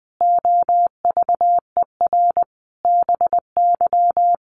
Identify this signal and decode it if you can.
Morse code, heard as a keyed tone, about 20 wpm, 700 Hz. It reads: OVER BY
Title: Sentences from Top 1400 Words 20wpm